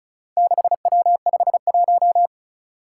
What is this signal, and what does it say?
Morse code, heard as a keyed tone, about 36 wpm, 700 Hz. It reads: 6W51
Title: Four Challenging Letter-Number Combinations 35wpm